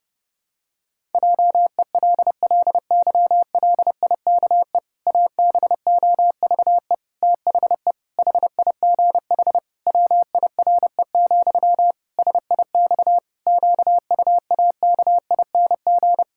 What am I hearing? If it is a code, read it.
Morse code, heard as a keyed tone, about 30 wpm, 700 Hz. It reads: JELLYLIKE ABOVE THE HIGH WIRE, SIX QUAKING
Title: Pangram Challenge v2 - 30wpm